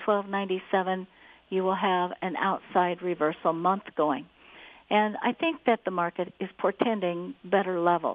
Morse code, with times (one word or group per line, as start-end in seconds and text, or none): none